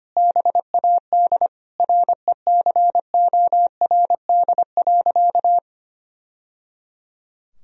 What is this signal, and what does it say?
Morse code, heard as a keyed tone, about 25 wpm, 700 Hz. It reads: BAD RECORD.
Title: Sentences from Top 1000 Words - Review Part 3 - 25wpm